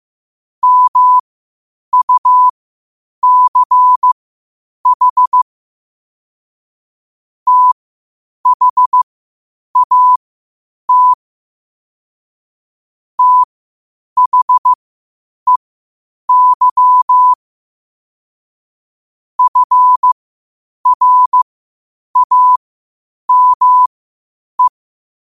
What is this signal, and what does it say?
Morse code, heard as a keyed tone, about 15 wpm, spaced as for 8 wpm, 1000 Hz. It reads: MUCH THAT THEY FRAME
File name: qotdfiles / 8wpm